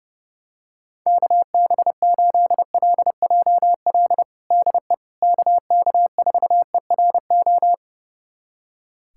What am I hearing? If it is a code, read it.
Morse code, heard as a keyed tone, about 32 wpm, 700 Hz. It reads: KB8LJL DE KK4ERO
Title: USA Callsign DE Callsign 30wpm